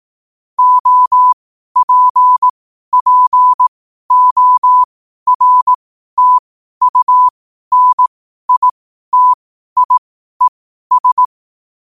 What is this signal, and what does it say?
Morse code, heard as a keyed tone, about 18 wpm, spaced as for 12 wpm, 1000 Hz. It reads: OPPORTUNITIES
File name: qotdfiles / 12wpm